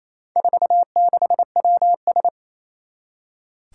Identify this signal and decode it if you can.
Morse code, heard as a keyed tone, about 28 wpm, 700 Hz. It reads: 46WS
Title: Four Challenging Letter-Number Combinations 28wpm